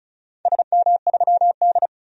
Morse code, heard as a keyed tone, about 36 wpm, 700 Hz. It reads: SM3D
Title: DXCC-100 Call Signs - 4 Characters - 35wpm